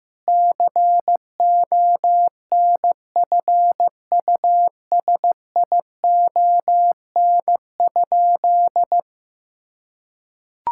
Morse code, heard as a keyed tone, about 15 wpm, 700 Hz. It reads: CONFUSION?
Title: Sentences from Top 1600 Words 15wpm